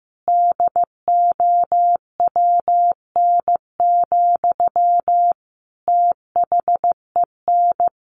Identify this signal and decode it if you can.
Morse code, heard as a keyed tone, about 15 wpm, 700 Hz. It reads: DOWN, THEN
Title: Sentences from Top 1400 Words 15wpm